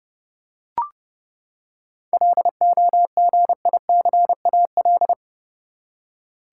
Morse code, heard as a keyed tone, about 32 wpm, 700 Hz. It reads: LOGICAL
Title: Seven Letter Words - 30wpm